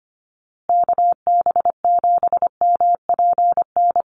Morse code, heard as a keyed tone, about 25 wpm, 700 Hz. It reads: KB7MPN
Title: USA Call Signs Repeated 3x 25wpm